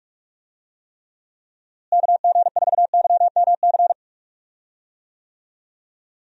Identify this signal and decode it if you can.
Morse code, heard as a keyed tone, about 42 wpm, 700 Hz. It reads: KG4YMC
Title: USA Call Signs - 6 Characters - Repeated 2x - 45wpm